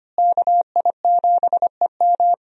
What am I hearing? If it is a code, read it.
Morse code, heard as a keyed tone, about 25 wpm, 700 Hz. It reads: KI7EM
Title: USA Call Signs - 5 Characters - Repeated 2x - 25wpm